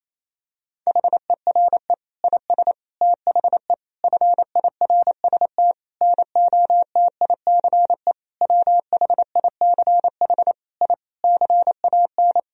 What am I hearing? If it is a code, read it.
Morse code, heard as a keyed tone, about 28 wpm, 700 Hz. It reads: HERE IS THE FIRST NOTICE WHICH I CAN
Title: Sentences from Top 1000 Words - Review Part 2 - 28wpm